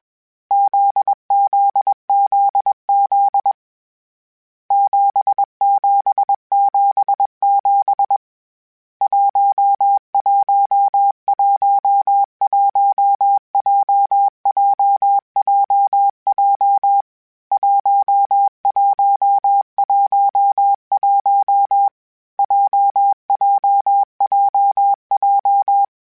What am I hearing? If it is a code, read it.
Morse code, heard as a keyed tone, about 21 wpm, 800 Hz. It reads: ZZZZ 7777 1111 JJJJ 1111 JJJJ
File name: Basic Sets / Basic Letter Groups Collective7 / JZ17@20